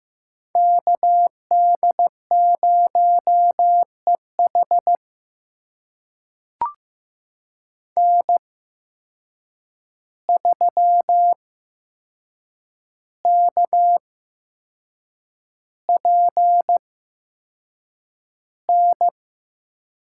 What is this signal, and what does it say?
Morse code, heard as a keyed tone, about 16 wpm, spaced as for 6 wpm, 700 Hz. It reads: KD0EH N3KPN
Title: USA Call Signs - 5 Characters - 8x Character Spacing - 15wpm